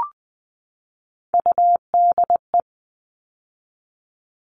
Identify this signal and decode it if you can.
Morse code, heard as a keyed tone, about 20 wpm, 700 Hz. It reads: UDE
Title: All Cut Numbers - 3-Digit 20wpm